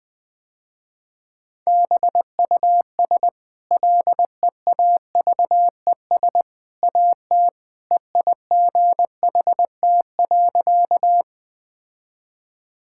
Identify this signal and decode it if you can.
Morse code, heard as a keyed tone, about 21 wpm, 700 Hz. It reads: BUS LEAVES AT EIGHT.
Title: Easy Sentences with Hints - 20wpm